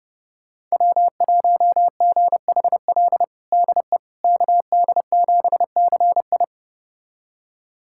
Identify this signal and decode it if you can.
Morse code, heard as a keyed tone, about 30 wpm, 700 Hz. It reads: W1GHL DE KD7CI
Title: USA Callsign DE Callsign - 5 Characters - 30wpm